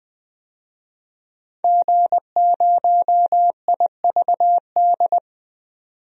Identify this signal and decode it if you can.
Morse code, heard as a keyed tone, about 21 wpm, 700 Hz. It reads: G0IVD
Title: DXCC-100 Call Signs - 5 Characters - 20wpm